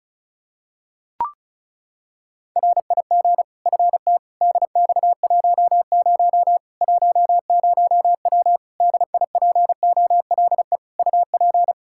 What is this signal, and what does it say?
Morse code, heard as a keyed tone, about 36 wpm, 700 Hz. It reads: RIG FT DX10 10W DIPOLE UP
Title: Practice QSO - Exchange 2 - Info - 35wpm